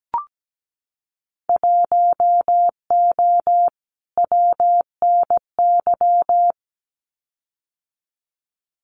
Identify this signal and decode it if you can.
Morse code, heard as a keyed tone, about 17 wpm, 700 Hz. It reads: 1O WNY
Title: Winter Field Day Exchanges Encore 17wpm